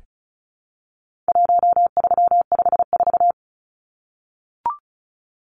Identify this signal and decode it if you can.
Morse code, heard as a keyed tone, about 34 wpm, 700 Hz. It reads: 1354
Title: Numbers - 4-Digit 35wpm